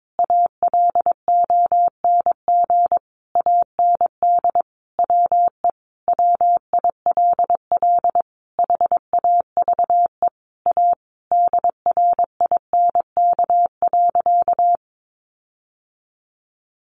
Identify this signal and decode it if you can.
Morse code, heard as a keyed tone, about 22 wpm, 700 Hz. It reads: ALONG AND WE WILL HAVE A DRINK.
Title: Sentences from Top 1500 Words 22wpm